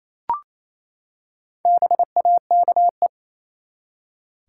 Morse code, heard as a keyed tone, about 28 wpm, 700 Hz. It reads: BAKE
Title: Four Letter Words - 28wpm